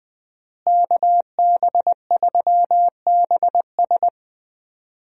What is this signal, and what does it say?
Morse code, heard as a keyed tone, about 20 wpm, 700 Hz. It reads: KB3BS
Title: USA Call Signs - 5 Characters - Repeated 3x - 20wpm